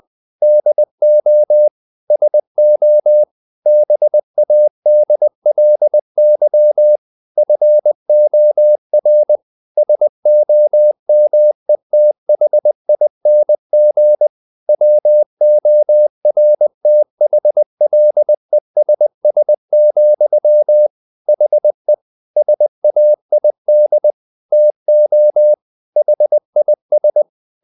Morse code, heard as a keyed tone, about 20 wpm, 600 Hz. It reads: DO SO BADLY FOR SOMETHING WORTHLESS, HE SAID TO HIS
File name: the old man and the sea part 2 20wpm